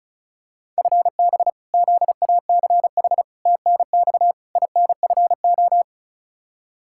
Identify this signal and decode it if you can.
Morse code, heard as a keyed tone, about 34 wpm, 700 Hz. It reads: FB ZACH TNX INFO
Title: K1USN SST QSO Practice 35wpm